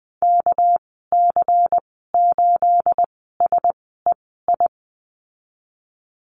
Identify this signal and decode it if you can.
Morse code, heard as a keyed tone, about 20 wpm, spaced as for 15 wpm, 700 Hz. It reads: KC8SEI
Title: 231 Lesson 37g - Letter x - Call Signs 20wpm